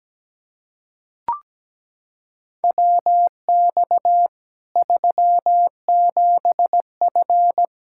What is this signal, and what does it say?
Morse code, heard as a keyed tone, about 17 wpm, 700 Hz. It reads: WX 37F
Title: Practice QSO - Exchange 2 - Info - 17wpm